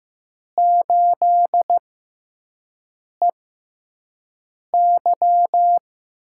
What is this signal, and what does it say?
Morse code, heard as a keyed tone, about 15 wpm, spaced as for 5 wpm, 700 Hz. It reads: 8EY
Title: USA Call Signs - 4 Characters - 6x Character Spacing - 15wpm